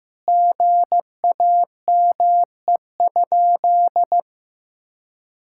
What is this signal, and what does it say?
Morse code, heard as a keyed tone, about 16 wpm, 700 Hz. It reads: GAME?
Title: Sentences from Top 1600 Words 15wpm